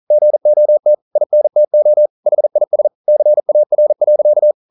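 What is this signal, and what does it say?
Morse code, heard as a keyed tone, about 41 wpm, 600 Hz. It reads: GOT INTO HIS CAR.
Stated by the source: Morse code practice at 41 wpm, from CW med Gnister.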